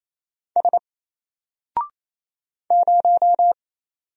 Morse code, heard as a keyed tone, about 27 wpm, 700 Hz. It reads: S 0
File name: Single-Letter-Number-28wpm